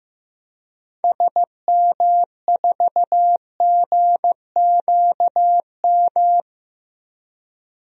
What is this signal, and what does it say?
Morse code, heard as a keyed tone, about 15 wpm, 700 Hz. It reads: SM4GQM
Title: DXCC-100 Call Signs - 6 Characters - 15wpm